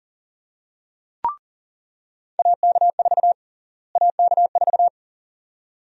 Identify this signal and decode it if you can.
Morse code, heard as a keyed tone, about 37 wpm, 700 Hz. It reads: AK4 AK4
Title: USA Call Sign Prefixes - Repeated 2x - 40wpm